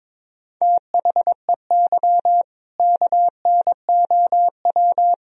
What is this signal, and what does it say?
Morse code, heard as a keyed tone, about 22 wpm, 700 Hz. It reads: THEY KNOW
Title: Sentences from Top 500 Words and Abbreviations 22wpm